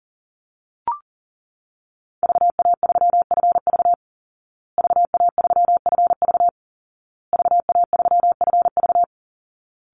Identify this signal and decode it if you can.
Morse code, heard as a keyed tone, about 40 wpm, 700 Hz. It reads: VA3FV VA3FV VA3FV
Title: Call Signs Repeated 2x - 40wpm